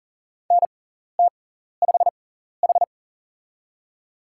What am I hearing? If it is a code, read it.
Morse code, heard as a keyed tone, about 40 wpm, spaced as for 14 wpm, 690 Hz. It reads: NT5H
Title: USA Call Signs - 4 Characters - 6x Character Spacing - 40wpm